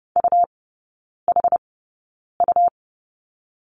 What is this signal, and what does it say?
Morse code, heard as a keyed tone, about 30 wpm, spaced as for 9 wpm, 700 Hz. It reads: UHU
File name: Lesson 17e - Letter u - Three Characters 30wpm